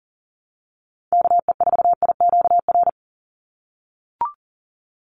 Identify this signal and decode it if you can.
Morse code, heard as a keyed tone, about 40 wpm, 700 Hz. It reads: KE4IQR
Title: USA Call Signs 40wpm